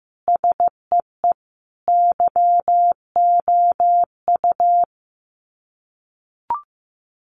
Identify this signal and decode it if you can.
Morse code, heard as a keyed tone, about 15 wpm, 700 Hz. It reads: SEE YOU
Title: Sets of 4 Words 15wpm